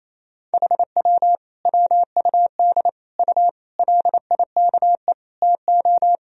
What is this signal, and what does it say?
Morse code, heard as a keyed tone, about 28 wpm, 700 Hz. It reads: HW WUD U LIKE TO
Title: Sentences from Top 100 Words and Abbreviations 28wpm